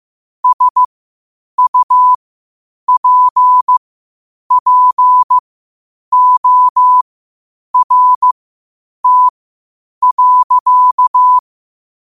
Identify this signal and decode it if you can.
Morse code, heard as a keyed tone, about 15 wpm, spaced as for 8 wpm, 1000 Hz. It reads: SUPPORT.
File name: qotdfiles / 8wpm